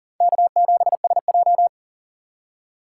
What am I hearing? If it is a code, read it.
Morse code, heard as a keyed tone, about 40 wpm, 700 Hz. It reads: K7SJ
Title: USA Call Signs - 4 Characters - 6x Character Spacing - 40wpm